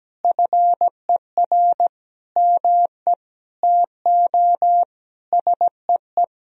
Morse code, heard as a keyed tone, about 17 wpm, 700 Hz. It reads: FER ME TO SEE
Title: Sentences from Top 500 Words and Abbreviations 17wpm